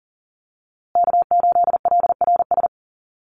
Morse code, heard as a keyed tone, about 40 wpm, 700 Hz. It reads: K8LRS
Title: USA Call Signs Repeated 2x 40wpm